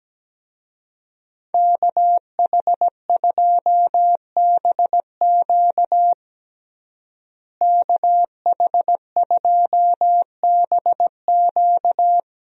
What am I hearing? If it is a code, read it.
Morse code, heard as a keyed tone, about 17 wpm, 700 Hz. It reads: KH2BQ KH2BQ
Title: USA Call Signs - 5 Characters - Repeated 2x - 17wpm